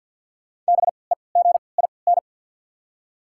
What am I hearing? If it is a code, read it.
Morse code, heard as a keyed tone, about 54 wpm, 690 Hz. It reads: B E G I N
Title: 3x Character Spacing - Top 300 Words - 50wpm